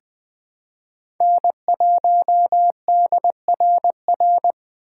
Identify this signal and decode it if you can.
Morse code, heard as a keyed tone, about 20 wpm, 700 Hz. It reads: N1DRR
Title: USA Call Signs - 5 Characters - 3x Character Spacing - 20wpm